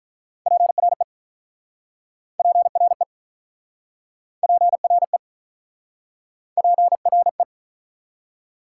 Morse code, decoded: PRE PRE PRE PRE

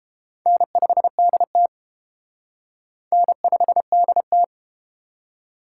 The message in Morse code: N5DT N5DT

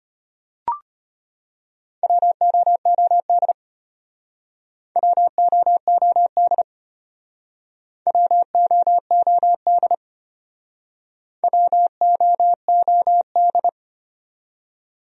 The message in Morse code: WOOD WOOD WOOD WOOD

